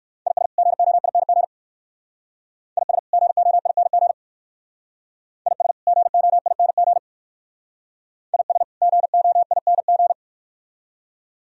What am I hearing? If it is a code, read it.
Morse code, heard as a keyed tone, about 57 wpm, 700 Hz. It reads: IS GOING IS GOING IS GOING IS GOING